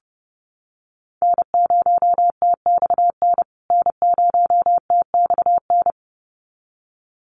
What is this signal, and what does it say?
Morse code, heard as a keyed tone, about 28 wpm, 700 Hz. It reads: N0TXN N0TXN
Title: USA Call Signs Repeated 2x 30wpm